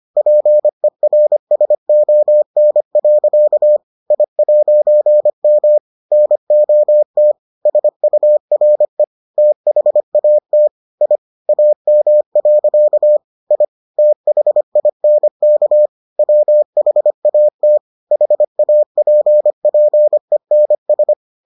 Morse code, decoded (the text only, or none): PERSON. I'M NOT SURE THAT I AM. I THINK WHAT HAPPENS